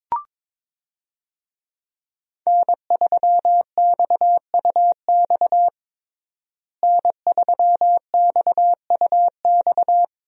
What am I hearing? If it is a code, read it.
Morse code, heard as a keyed tone, about 22 wpm, 700 Hz. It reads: N3XUX N3XUX
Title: USA Call Signs - 5 Characters - Repeated 2x - 22wpm